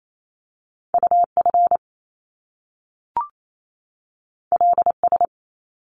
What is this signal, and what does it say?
Morse code, heard as a keyed tone, about 28 wpm, 700 Hz. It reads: UF LS
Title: Top Two Letter Combinations in Words 28wpm